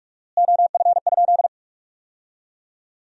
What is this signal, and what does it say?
Morse code, heard as a keyed tone, about 47 wpm, 700 Hz. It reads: OF?
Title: 4x Word-Spacing - Sentences from Top 200 Words - 45wpm